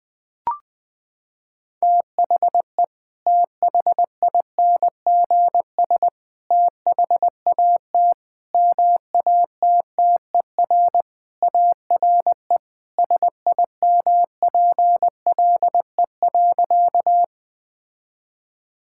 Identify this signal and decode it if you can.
Morse code, heard as a keyed tone, about 20 wpm, 700 Hz. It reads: THE THINGS THAT MATTER ARE SIMPLE.
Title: Sentences with 3-Word Phrases - 20wpm